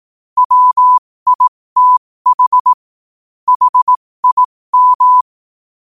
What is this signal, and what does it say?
Morse code, heard as a keyed tone, about 18 wpm, spaced as for 15 wpm, 1000 Hz. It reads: WITH HIM
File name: qotdfiles / 15wpm